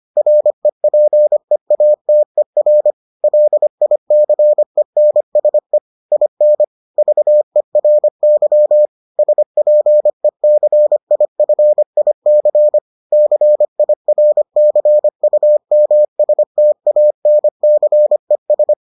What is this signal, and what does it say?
Morse code, heard as a keyed tone, about 25 wpm, 600 Hz. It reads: REPEATER LICENSE IN VERY SPECIFIC CIRCUMSTANCES